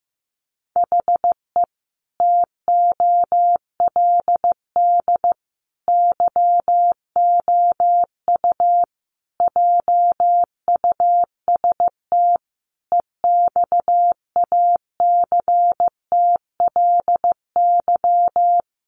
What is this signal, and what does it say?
Morse code, heard as a keyed tone, about 15 wpm, 700 Hz. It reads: HE TOLD YOU JUST EXACTLY
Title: Sentences from Top 900 Words 15wpm